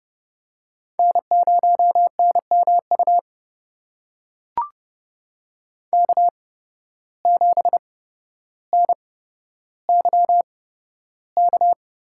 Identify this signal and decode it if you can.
Morse code, heard as a keyed tone, about 30 wpm, spaced as for 13 wpm, 700 Hz. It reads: N0NMU K7NYK
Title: USA Call Signs - 5 Characters - 8x Character Spacing - 30wpm